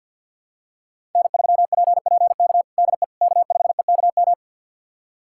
Morse code, heard as a keyed tone, about 50 wpm, 700 Hz. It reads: N3PPK DE K5EKM
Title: USA Callsign DE Callsign - 5 Characters - 50wpm